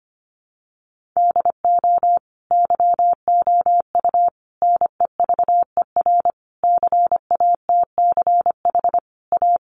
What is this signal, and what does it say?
Morse code, heard as a keyed tone, about 25 wpm, 700 Hz. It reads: DO YOU NEVER CATCH A
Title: Sentences from Top 1500 Words 25wpm